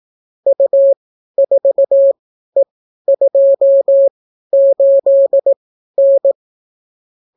Morse code, heard as a keyed tone, about 18 wpm, spaced as for 12 wpm, 550 Hz. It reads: U4E28N